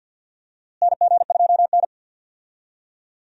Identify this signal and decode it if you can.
Morse code, heard as a keyed tone, about 47 wpm, 700 Hz. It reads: NG2N
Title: USA Call Signs - 4 Characters - Repeated 2x - 50wpm